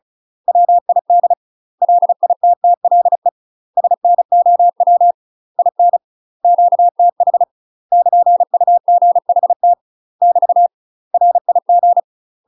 Morse code, heard as a keyed tone, about 36 wpm, 700 Hz. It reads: WID LITTLE SNOW IN QTH <KN>UGHT <BT> RIG
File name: UK MASTER QSOs TRAINING / 35WPM / UK MASTER QSOs 0022